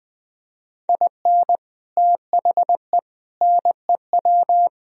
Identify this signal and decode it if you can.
Morse code, heard as a keyed tone, about 20 wpm, 700 Hz. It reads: IN THE NEW
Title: Speed-Racing - Sets of 5 Words 20wpm